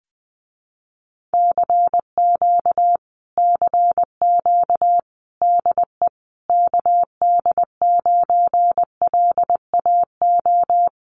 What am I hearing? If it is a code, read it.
Morse code, heard as a keyed tone, about 20 wpm, 700 Hz. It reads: CQ CQ DE KD9LAO
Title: Common QSO Phrases Encore 20wpm